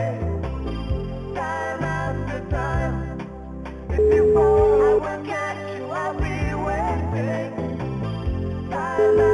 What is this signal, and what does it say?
Ringback tone (Europe, 425 Hz), with noise.